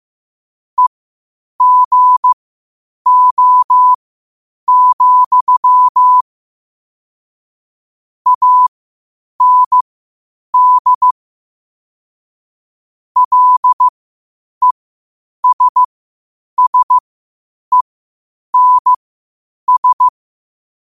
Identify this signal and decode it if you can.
Morse code, heard as a keyed tone, about 15 wpm, spaced as for 8 wpm, 1000 Hz. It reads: EGO, AND LESSENS
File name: qotdfiles / 8wpm